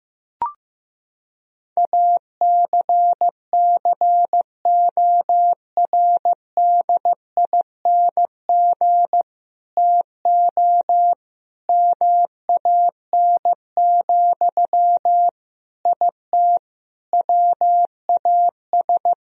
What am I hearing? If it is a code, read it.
Morse code, heard as a keyed tone, about 15 wpm, 700 Hz. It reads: ACCORDING TO MAN, IT WAS
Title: Sentences from Top 1000 Words - Review Part 4 - 15wpm